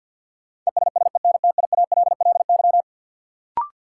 Morse code, heard as a keyed tone, about 48 wpm, 700 Hz. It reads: ESSENTIALLY
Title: Sentences from Top 1300 Words - New Word Drill Alt - 50wpm